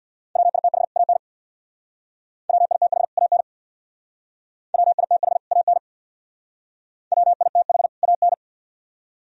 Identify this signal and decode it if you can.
Morse code, heard as a keyed tone, about 62 wpm, 700 Hz. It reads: WITH AN WITH AN WITH AN WITH AN
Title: Speed-Racing - Sets of 2 Words 50wpm